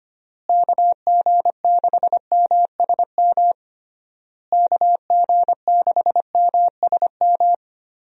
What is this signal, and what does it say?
Morse code, heard as a keyed tone, about 25 wpm, 700 Hz. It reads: KG6MSM KG6MSM
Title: USA Call Signs - 6 Characters - Repeated 3x - 25wpm